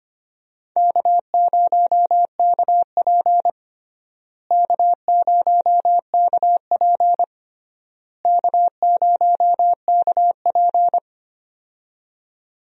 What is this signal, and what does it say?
Morse code, heard as a keyed tone, about 25 wpm, 700 Hz. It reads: K0KP K0KP K0KP
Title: USA Call Signs - 4 Characters - Repeated 3x - 25wpm